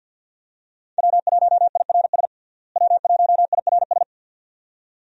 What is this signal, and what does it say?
Morse code, heard as a keyed tone, about 51 wpm, 700 Hz. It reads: W1IRS W1IRS
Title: USA Call Signs - 5 Characters - Repeated 2x - 50wpm